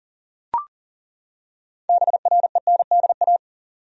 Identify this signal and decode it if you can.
Morse code, heard as a keyed tone, about 40 wpm, 700 Hz. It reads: BRENDA
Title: USA Common Names - 40wpm